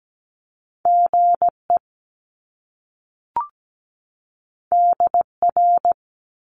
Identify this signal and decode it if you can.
Morse code, heard as a keyed tone, about 17 wpm, 700 Hz. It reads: GE DR
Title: Top Two Letter Combinations in Words 17wpm